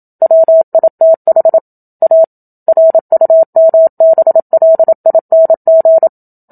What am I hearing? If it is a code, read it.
Morse code, heard as a keyed tone, about 27 wpm, 650 Hz. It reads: WITH A RUMBLING
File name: Book Two / 01-Chapter 1 - 27 WPM